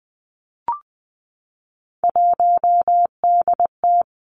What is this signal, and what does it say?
Morse code, heard as a keyed tone, about 20 wpm, 700 Hz. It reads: 1DT
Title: Three Challenging Letter-Number Combinations 20wpm